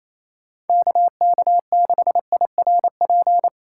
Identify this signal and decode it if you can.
Morse code, heard as a keyed tone, about 28 wpm, 700 Hz. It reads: KK6IRP
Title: USA Call Signs - 6 Characters - 2x Character Spacing - 28wpm